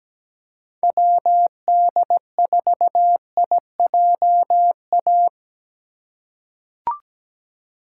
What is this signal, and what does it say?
Morse code, heard as a keyed tone, about 17 wpm, 700 Hz. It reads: WD4IJA
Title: USA Call Signs - 6 Characters - 2x Character Spacing - 17wpm